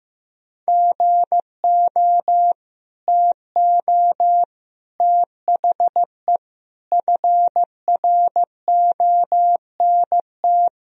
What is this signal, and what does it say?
Morse code, heard as a keyed tone, about 16 wpm, 700 Hz. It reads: GO TO THE FRONT
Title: Sentences from Top 1000 Words - Review Part 2 - 15wpm